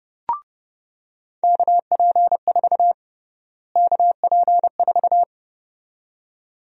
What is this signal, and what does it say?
Morse code, heard as a keyed tone, about 30 wpm, 700 Hz. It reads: KP4 KP4
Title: USA Call Sign Prefixes - Repeated 2x - 30wpm